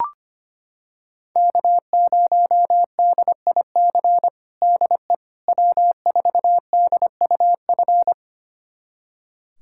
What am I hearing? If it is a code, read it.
Morse code, heard as a keyed tone, about 25 wpm, 700 Hz. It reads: K0DIC DE W4DUF
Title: USA Callsign DE Callsign - 5 Characters - 25wpm